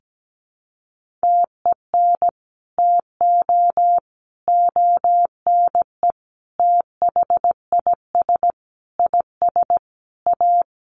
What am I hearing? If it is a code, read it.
Morse code, heard as a keyed tone, about 17 wpm, 700 Hz. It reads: TEN TO ONE THIS IS A
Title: Sentences from Top 1500 Words 17wpm